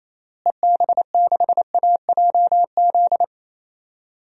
Morse code, heard as a keyed tone, about 28 wpm, 700 Hz. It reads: EB6AJZ
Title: DXCC-100 Call Signs - 6 Characters - Repeated 3x - 28wpm